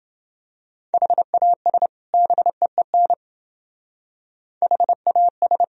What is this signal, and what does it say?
Morse code, heard as a keyed tone, about 29 wpm, 700 Hz. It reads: HAS BEEN HAS